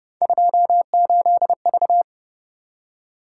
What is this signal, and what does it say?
Morse code, heard as a keyed tone, about 29 wpm, 700 Hz. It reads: 28V
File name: Speed-Racing - Three Letter-Number Combinations - 25wpm